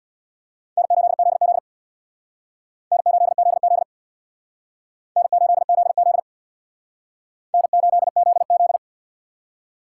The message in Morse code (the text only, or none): N8ZZ N8ZZ N8ZZ N8ZZ